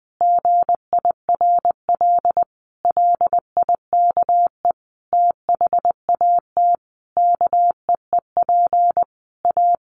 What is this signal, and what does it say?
Morse code, heard as a keyed tone, about 20 wpm, 700 Hz. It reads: GIRL LIKE THAT KEEP A
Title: Sentences from Top 1200 Words 20wpm